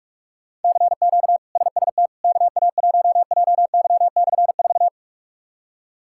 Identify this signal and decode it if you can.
Morse code, heard as a keyed tone, about 46 wpm, 700 Hz. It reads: CQ SST KA1JY/4
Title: K1USN SST QSO Practice - 45wpm